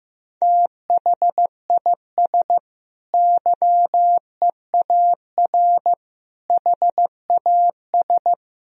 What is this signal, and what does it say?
Morse code, heard as a keyed tone, about 15 wpm, 700 Hz. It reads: THIS YEAR HAS
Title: Sentences from Top 1000 Words - Review Part 3 - 15wpm